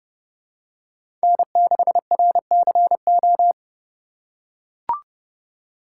Morse code, heard as a keyed tone, about 29 wpm, 700 Hz. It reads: N6RCO